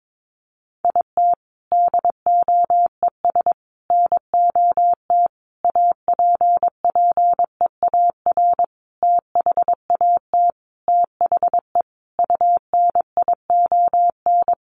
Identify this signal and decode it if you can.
Morse code, heard as a keyed tone, about 22 wpm, 700 Hz. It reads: IT DOES NOT APPEAR THAT THE UNION
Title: Sentences from Top 1400 Words 22wpm